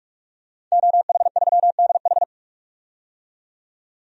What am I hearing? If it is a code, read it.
Morse code, heard as a keyed tone, about 43 wpm, 700 Hz. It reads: OH3DH
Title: DXCC-100 Call Signs - 5 Characters - 45wpm